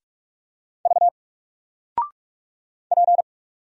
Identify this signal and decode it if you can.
Morse code, heard as a keyed tone, about 44 wpm, 700 Hz. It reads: V P
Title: Single Character (Letter:Number:Punctuation:Prosign) 45wpm